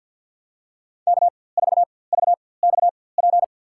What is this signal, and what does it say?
Morse code, heard as a keyed tone, about 53 wpm, spaced as for 24 wpm, 700 Hz. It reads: K4VXP
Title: USA Call Signs - 5 Characters - 4x Character Spacing - 50wpm